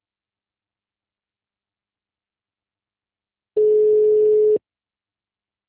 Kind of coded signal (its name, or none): telephone call progress tone